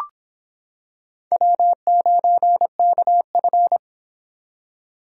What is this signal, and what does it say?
Morse code, heard as a keyed tone, about 25 wpm, 700 Hz. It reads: W9KF